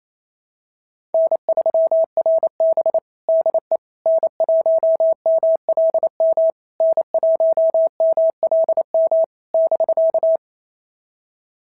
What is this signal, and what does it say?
Morse code, heard as a keyed tone, about 27 wpm, 660 Hz. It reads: N3RB DE N1MLM N1MLM <BK>
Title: SOTA QSO Practice 28wpm